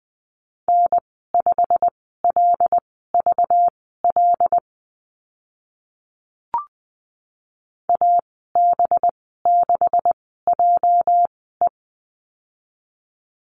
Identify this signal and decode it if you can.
Morse code, heard as a keyed tone, about 20 wpm, spaced as for 14 wpm, 700 Hz. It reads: N5LVL AB6JE
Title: 207 Lesson 33g - Letter j - Call Signs 20wpm